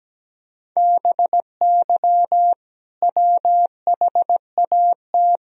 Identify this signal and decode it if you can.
Morse code, heard as a keyed tone, about 17 wpm, 700 Hz. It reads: BY WHAT
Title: Sentences from Top 1000 Words - Review Part 3 - 17wpm